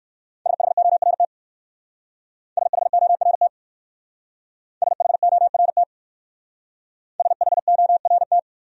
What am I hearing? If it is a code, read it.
Morse code, heard as a keyed tone, about 56 wpm, 700 Hz. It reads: SHORT SHORT SHORT SHORT
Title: Speed-Racing Top 500 Words 45wpm